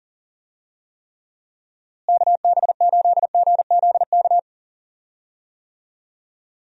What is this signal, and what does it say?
Morse code, heard as a keyed tone, about 38 wpm, 700 Hz. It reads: KB8GZK
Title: USA Call Signs - 6 Characters - Repeated 2x - 40wpm